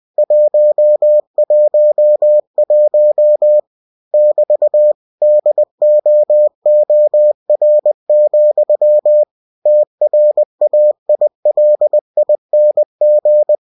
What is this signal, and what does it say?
Morse code, heard as a keyed tone, about 20 wpm, 600 Hz. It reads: 111 <BT> DOOR, TRAILING
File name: cw-books / winnie-the-pooh / winnie-20wpm-page-0111